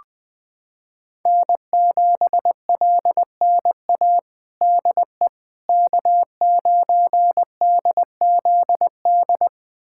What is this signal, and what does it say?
Morse code, heard as a keyed tone, about 21 wpm, 700 Hz. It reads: N7LNA DE K9DZD
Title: USA Callsign DE Callsign - 5 Characters - 20wpm